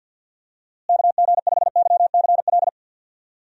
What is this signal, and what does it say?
Morse code, heard as a keyed tone, about 47 wpm, 700 Hz. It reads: KG5YCL
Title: USA Call Signs - 6 Characters - Repeated 3x - 50wpm